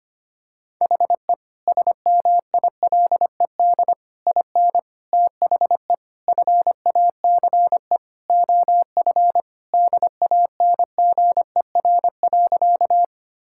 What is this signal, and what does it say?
Morse code, heard as a keyed tone, about 25 wpm, 700 Hz. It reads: HE SMILED IN THE FACE OF DANGER.
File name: practice / 25wpm / Sentences with 4-Word Phrases - 25wpm